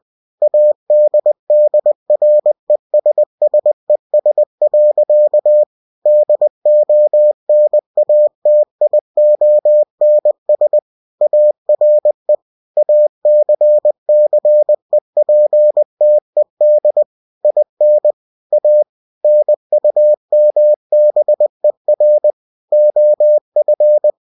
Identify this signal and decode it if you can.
Morse code, heard as a keyed tone, about 20 wpm, 600 Hz. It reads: ADDRESSES. DONATIONS ARE ACCEPTED IN A NUMBER OF
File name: cw-books / winnie-the-pooh / winnie-20wpm-page-0122